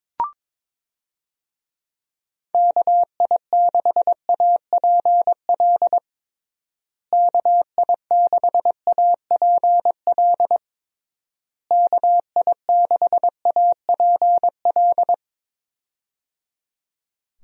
Morse code, heard as a keyed tone, about 22 wpm, 700 Hz. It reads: KI6APL KI6APL KI6APL
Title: USA Call Signs - 6 Characters - Repeated 3x - 22wpm